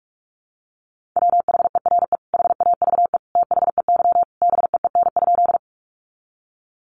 Morse code, heard as a keyed tone, about 44 wpm, 700 Hz. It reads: WHERE HAVE THEY BEEN?
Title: Sentences from Top 200 Words 45wpm